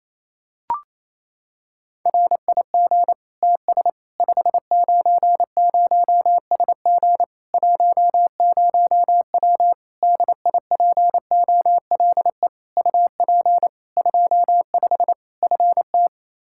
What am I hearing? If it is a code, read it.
Morse code, heard as a keyed tone, about 28 wpm, 700 Hz. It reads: RIG TS 590SG 10W DIPOLE UP 25 FT
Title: K1USN SST QSO Practice 28wpm